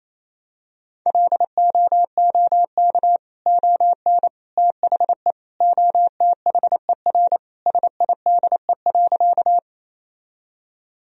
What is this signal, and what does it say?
Morse code, heard as a keyed tone, about 28 wpm, 700 Hz. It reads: LOOK ON THE OTHER SIDE.
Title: Sentences from Top 1000 Words - Review Part 1 - 28wpm